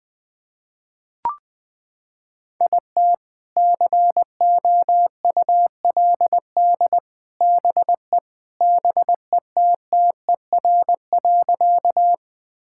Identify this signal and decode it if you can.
Morse code, heard as a keyed tone, about 21 wpm, 700 Hz. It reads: IT COULD BE BETTER.
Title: Sentences with 2-Word Phrases - 20wpm